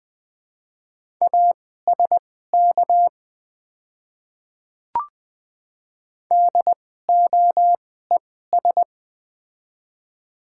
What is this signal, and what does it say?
Morse code, heard as a keyed tone, about 21 wpm, spaced as for 15 wpm, 700 Hz. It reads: ASK DOES
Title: 2x Character Spacing - Top 200 Words - 20wpm